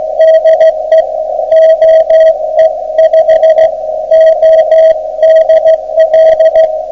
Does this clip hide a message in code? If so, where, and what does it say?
Yes: Morse code from 0.2 to 6.7 s, heard as a keyed tone, about 15 wpm, 640 Hz, with radio noise. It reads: DE OE5ODL